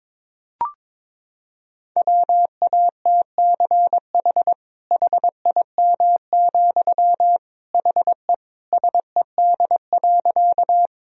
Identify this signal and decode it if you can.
Morse code, heard as a keyed tone, about 22 wpm, 700 Hz. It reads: WATCH HIM, HE SED.
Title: Sentences from Top 300 Words and Abbreviations 22wpm